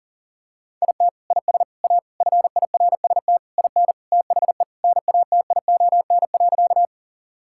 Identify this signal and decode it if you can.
Morse code, heard as a keyed tone, about 40 wpm, 700 Hz. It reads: IT IS A FIRST IN THE NATION.
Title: Sentences from Top 1000 Words - Review Part 6 - 40wpm